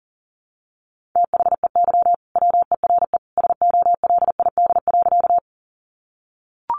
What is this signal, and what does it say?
Morse code, heard as a keyed tone, about 40 wpm, 700 Hz. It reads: THEY WERE SOLID.
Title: Sentences from Top 1500 Words 40wpm